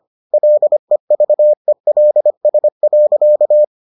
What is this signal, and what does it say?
Morse code, heard as a keyed tone, about 25 wpm, 600 Hz. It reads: LEVELS.